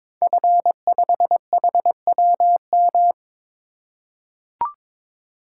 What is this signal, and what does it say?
Morse code, heard as a keyed tone, about 22 wpm, 700 Hz. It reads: F5HWM
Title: Code Groups - 5 Letter:Numbers - 22wpm